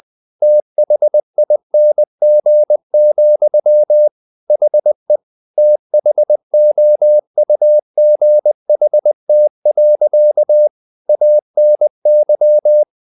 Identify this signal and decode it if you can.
Morse code, heard as a keyed tone, about 20 wpm, 600 Hz. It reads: THING, HE THOUGHT. ANY
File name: the old man and the sea part 2 20wpm